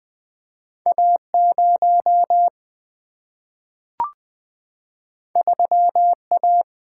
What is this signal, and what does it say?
Morse code, heard as a keyed tone, about 20 wpm, 700 Hz. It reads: A0 3A
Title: Single-Letter Single-Number 20wpm